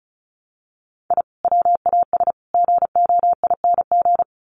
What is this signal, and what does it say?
Morse code, heard as a keyed tone, about 34 wpm, 700 Hz. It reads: I WAS GOING
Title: Sets of 5 Words 35wpm